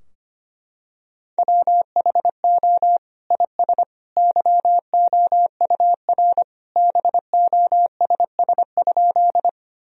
Morse code, heard as a keyed tone, about 25 wpm, 700 Hz. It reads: WHO IS YOUR BOSS?
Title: Sentences from Top 1000 Words - Review Part 4 - 25wpm